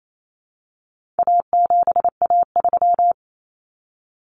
Morse code, heard as a keyed tone, about 28 wpm, 700 Hz. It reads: A7A3
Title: Common Cut Numbers - 4-Digit 28wpm